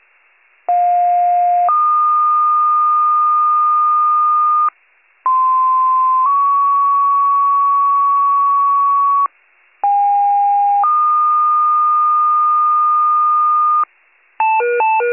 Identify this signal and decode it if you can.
Two-tone page, heard as a one-second tone, then a three-second tone, about 710 then 1220 Hz, about 1040 then 1120 Hz, about 810 then 1220 Hz. It is heard from 0.7 to 13.8 s.